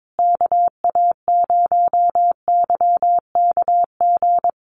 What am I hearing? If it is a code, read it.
Morse code, heard as a keyed tone, about 22 wpm, 700 Hz. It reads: KA0YKG
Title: USA Call Signs Repeated 3x 22wpm